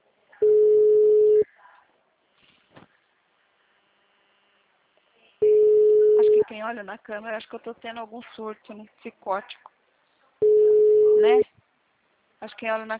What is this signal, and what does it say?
Ringback tone (Europe, 425 Hz), with speech.
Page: Itatiba - SP